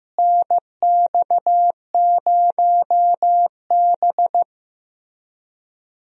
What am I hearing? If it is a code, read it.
Morse code, heard as a keyed tone, about 15 wpm, 700 Hz. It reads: NX0B
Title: USA Call Signs - 4 Characters - Repeated 3x - 15wpm